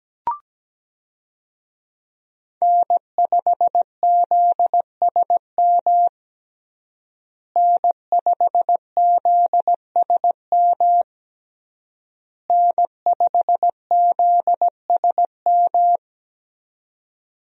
Morse code, heard as a keyed tone, about 17 wpm, 700 Hz. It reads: N5ZSM N5ZSM N5ZSM
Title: USA Call Signs - 5 Characters - Repeated 3x - 17wpm